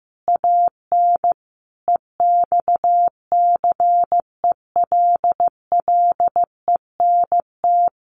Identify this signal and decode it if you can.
Morse code, heard as a keyed tone, about 15 wpm, 700 Hz. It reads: AN EXCELLENT
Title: Sentences from Top 1000 Words 15wpm